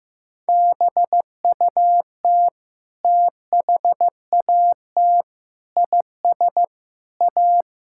Morse code, heard as a keyed tone, about 15 wpm, 700 Hz. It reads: BUT THAT IS A
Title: Sentences from Top 1600 Words 15wpm